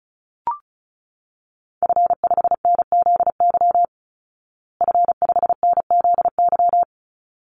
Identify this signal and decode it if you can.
Morse code, heard as a keyed tone, about 34 wpm, 700 Hz. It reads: F5NZY F5NZY
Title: Call Signs Repeated 3x - 35wpm